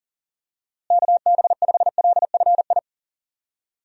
Morse code, heard as a keyed tone, about 40 wpm, 700 Hz. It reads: KB5LFI
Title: USA Call Signs - 6 Characters - Repeated 3x - 40wpm